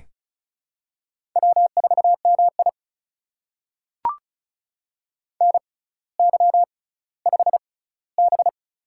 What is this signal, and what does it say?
Morse code, heard as a keyed tone, about 36 wpm, spaced as for 18 wpm, 700 Hz. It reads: W4MI NY5B
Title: USA Call Signs - 4 Characters - 6x Character Spacing - 35wpm